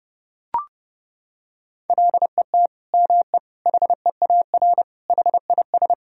Morse code, heard as a keyed tone, about 30 wpm, 700 Hz. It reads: LET ME HEAR HIS
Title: Sentences from Top 1000 Words - Review Part 3 - 30wpm